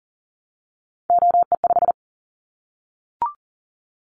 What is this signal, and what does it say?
Morse code, heard as a keyed tone, about 40 wpm, 700 Hz. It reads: OE5
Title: Callsign Prefixes for Top 50 DXCC - 40wpm